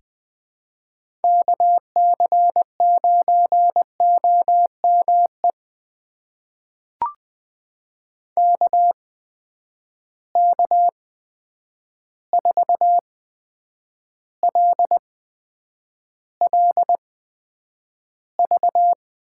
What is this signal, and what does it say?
Morse code, heard as a keyed tone, about 20 wpm, spaced as for 9 wpm, 700 Hz. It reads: KC9OME KK4LLV
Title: USA Call Signs - 6 Characters - 8x Character Spacing - 20wpm